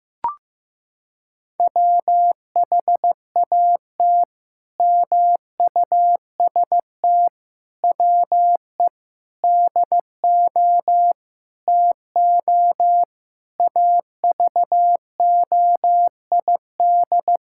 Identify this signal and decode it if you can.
Morse code, heard as a keyed tone, about 15 wpm, 700 Hz. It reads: WHAT MUST WE DO TO AVOID
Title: Sentences from Top 1000 Words - Review Part 6 - 15wpm